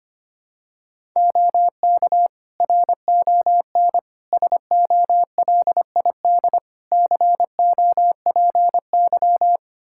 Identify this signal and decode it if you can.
Morse code, heard as a keyed tone, about 25 wpm, 700 Hz. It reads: OK RON SOLID COPY
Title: Practice QSO - Exchange 3 - More Info - 25wpm